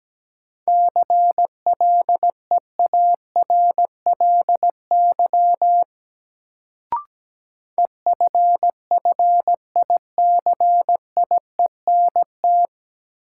Morse code, heard as a keyed tone, about 17 wpm, 700 Hz. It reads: CLEARLY EFFICIENT
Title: Sentences from Top 1300 Words - New Word Drill - 17wpm